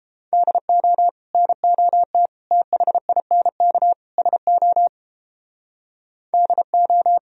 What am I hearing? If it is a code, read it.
Morse code, heard as a keyed tone, about 31 wpm, 700 Hz. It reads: DO NOT THINK SO DO